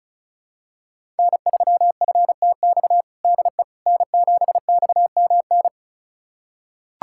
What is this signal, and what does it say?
Morse code, heard as a keyed tone, about 34 wpm, 700 Hz. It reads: N3FTX DE N7XMN
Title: USA Callsign DE Callsign - 5 Characters - 35wpm